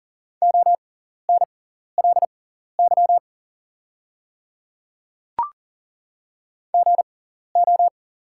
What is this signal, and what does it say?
Morse code, heard as a keyed tone, about 40 wpm, spaced as for 14 wpm, 690 Hz. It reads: ONLY GO
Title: 6x Character Spacing - Top 200 Words - 40wpm